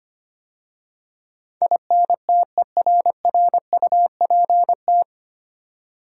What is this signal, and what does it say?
Morse code, heard as a keyed tone, about 25 wpm, 700 Hz. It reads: INTERRUPT
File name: practice / 25wpm / 9 Letter Words - First 3 Letters Given - 25wpm